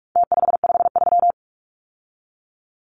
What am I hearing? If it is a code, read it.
Morse code, heard as a keyed tone, about 44 wpm, 700 Hz. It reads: T553
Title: Numbers - Contest Style - 4-Digit 45wpm